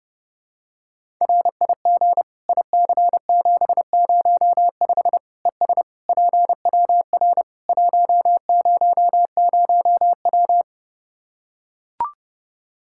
Morse code, heard as a keyed tone, about 30 wpm, 700 Hz. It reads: RIG IC705 ES PWR 100W
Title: Practice QSO - Exchange 2 - Info - 30wpm